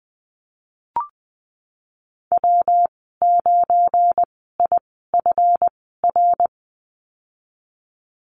Morse code, heard as a keyed tone, about 20 wpm, spaced as for 14 wpm, 700 Hz. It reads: W9IFR
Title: 245 Lesson 39g - Letter z - Call Signs 20wpm